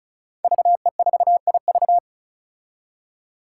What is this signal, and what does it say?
Morse code, heard as a keyed tone, about 36 wpm, 700 Hz. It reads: VE4IV
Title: Five Challenging Letter-Number Combinations 35wpm